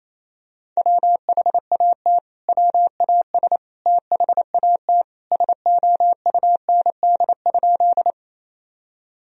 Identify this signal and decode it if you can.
Morse code, heard as a keyed tone, about 28 wpm, 700 Hz. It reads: WHAT WAS THAT SOUND?
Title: Sentences from Top 1000 Words - Review Part 1 - 28wpm